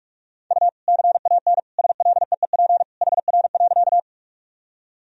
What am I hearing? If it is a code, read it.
Morse code, heard as a keyed tone, about 46 wpm, 700 Hz. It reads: U CAN SLEEP HR.
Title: Sentences from Top 500 Words and Abbreviations 45wpm